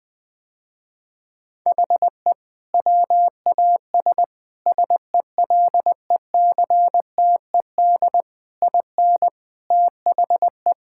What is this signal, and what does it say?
Morse code, heard as a keyed tone, about 20 wpm, 700 Hz. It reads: HE WAS SELECTED IN THE
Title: Sentences with 6-Word Phrases - Spoken Beginning - 20wpm